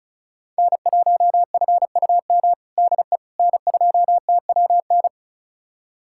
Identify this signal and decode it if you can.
Morse code, heard as a keyed tone, about 36 wpm, 700 Hz. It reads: N1FUM DE N2TWN
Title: USA Callsign DE Callsign - 5 Characters - 35wpm